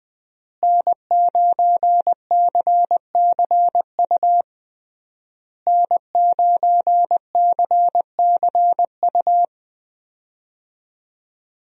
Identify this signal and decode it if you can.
Morse code, heard as a keyed tone, about 20 wpm, 700 Hz. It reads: N9CCU N9CCU
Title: USA Call Signs - 5 Characters - Repeated 3x - 20wpm